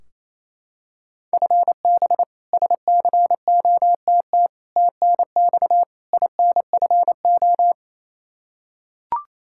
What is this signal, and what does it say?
Morse code, heard as a keyed tone, about 28 wpm, 700 Hz. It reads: FB SCOTT TNX INFO
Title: K1USN SST QSO Practice 28wpm